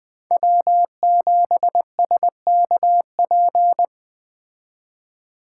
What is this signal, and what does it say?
Morse code, heard as a keyed tone, about 20 wpm, 700 Hz. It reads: W7SKP
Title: Speed Racing - USA Call Signs 17wpm